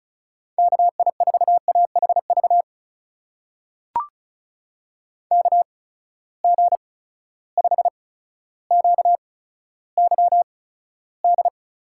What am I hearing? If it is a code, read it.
Morse code, heard as a keyed tone, about 36 wpm, spaced as for 15 wpm, 700 Hz. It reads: KI4AHV KG5QYD
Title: USA Call Signs - 6 Characters - 8x Character Spacing - 35wpm